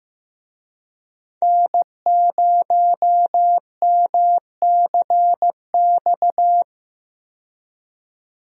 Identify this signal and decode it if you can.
Morse code, heard as a keyed tone, about 15 wpm, 700 Hz. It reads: N0MCX
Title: USA Call Signs - 5 Characters - 15wpm